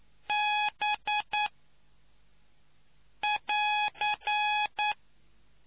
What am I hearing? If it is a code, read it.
Morse code, heard as a keyed tone, about 9 wpm, spaced as for 4 wpm, 840 Hz. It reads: B<AR>